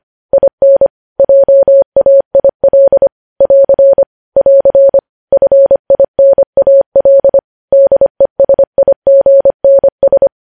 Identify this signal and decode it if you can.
Morse code, heard as a keyed tone, about 25 wpm, 550 Hz. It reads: IN JAIL <AR> <AR> FINAL DESIGNS
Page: news headlines in morse code at 25 wpm